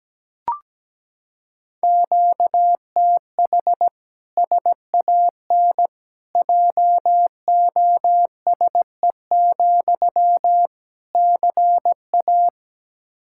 Morse code, heard as a keyed tone, about 17 wpm, 700 Hz. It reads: QTH SAN JOSE, CA
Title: Practice QSO - Exchange 1 - Beginning - 17wpm